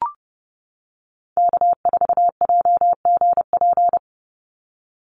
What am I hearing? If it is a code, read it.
Morse code, heard as a keyed tone, about 30 wpm, 700 Hz. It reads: K4JGP
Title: USA Call Signs 30wpm